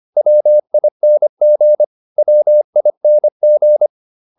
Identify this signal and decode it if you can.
Morse code, heard as a keyed tone, about 25 wpm, 600 Hz. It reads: WING WING